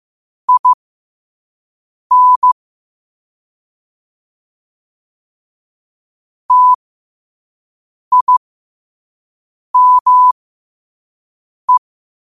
Morse code, heard as a keyed tone, about 15 wpm, spaced as for 5 wpm, 1000 Hz. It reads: IN TIME
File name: qotdfiles / 5wpm